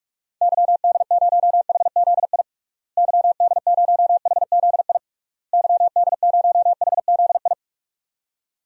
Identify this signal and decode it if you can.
Morse code, heard as a keyed tone, about 46 wpm, 700 Hz. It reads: YD0HZI YD0HZI YD0HZI
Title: DXCC-100 Call Signs - 6 Characters - Repeated 3x - 45wpm